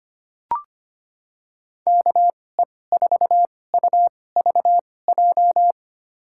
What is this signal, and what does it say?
Morse code, heard as a keyed tone, about 25 wpm, spaced as for 18 wpm, 700 Hz. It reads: KE4UVJ
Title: USA Call Signs - 6 Characters - 2x Character Spacing - 25wpm